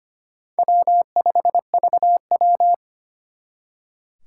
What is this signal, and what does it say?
Morse code, heard as a keyed tone, about 32 wpm, 700 Hz. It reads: W5VW